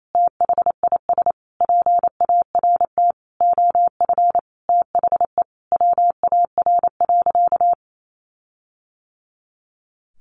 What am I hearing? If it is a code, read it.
Morse code, heard as a keyed tone, about 28 wpm, 700 Hz. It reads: THIS PART OF THE WAR.
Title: Sentences from Top 1500 Words 28wpm